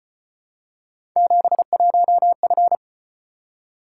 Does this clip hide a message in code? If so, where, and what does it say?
Yes: Morse code from 1.2 to 2.8 s, heard as a keyed tone, about 33 wpm, 700 Hz. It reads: 71F